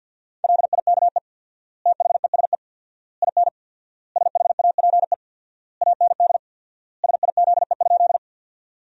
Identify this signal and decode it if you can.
Morse code, heard as a keyed tone, about 52 wpm, 700 Hz. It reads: LIKE THESE IN SHAPE AND SIZE?
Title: 4x Word-Spacing - Sentences from Top 500 Words - 50wpm